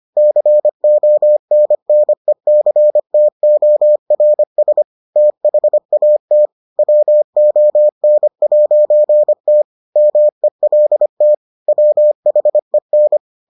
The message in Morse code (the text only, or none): CONNECTORS THAT WON'T MELT WHEN